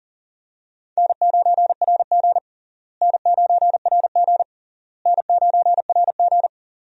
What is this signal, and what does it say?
Morse code, heard as a keyed tone, about 40 wpm, 700 Hz. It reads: N9RG N9RG N9RG
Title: USA Call Signs - 4 Characters - Repeated 3x - 40wpm